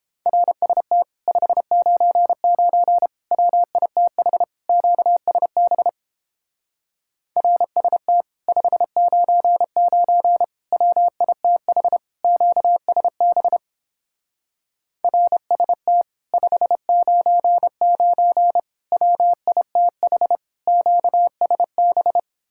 Morse code, heard as a keyed tone, about 29 wpm, 700 Hz. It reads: RST 599 WITH QSB RST 599 WITH QSB RST 599 WITH QSB